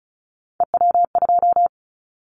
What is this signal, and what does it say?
Morse code, heard as a keyed tone, about 36 wpm, 700 Hz. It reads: EW2
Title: Callsign Prefixes for Top 50 DXCC - 35wpm